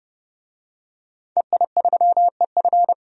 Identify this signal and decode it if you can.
Morse code, heard as a keyed tone, about 30 wpm, 700 Hz. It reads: EI3EF
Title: DXCC-100 Call Signs - 5 Characters - 30wpm